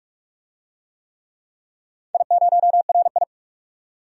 Morse code, heard as a keyed tone, about 44 wpm, 700 Hz. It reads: I0RI
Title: DXCC-100 Call Signs - 4 Characters - 45wpm